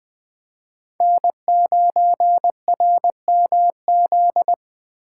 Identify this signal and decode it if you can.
Morse code, heard as a keyed tone, about 25 wpm, 700 Hz. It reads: N9RMZ
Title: Speed Racing - USA 1x3 Call Signs 20wpm